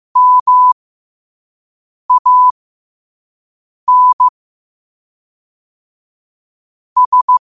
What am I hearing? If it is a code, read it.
Morse code, heard as a keyed tone, about 15 wpm, spaced as for 5 wpm, 1000 Hz. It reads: MAN S
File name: qotdfiles / 5wpm